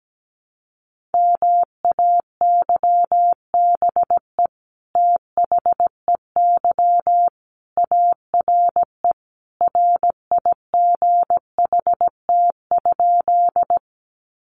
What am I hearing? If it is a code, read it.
Morse code, heard as a keyed tone, about 17 wpm, 700 Hz. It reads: MAYBE THEY ARE RIGHT?
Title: Sentences from Top 1000 Words 17wpm